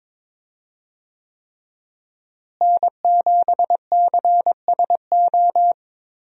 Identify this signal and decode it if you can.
Morse code, heard as a keyed tone, about 22 wpm, 700 Hz. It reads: N7CSO
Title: USA Call Signs - 5 Characters - Repeated 3x - 22wpm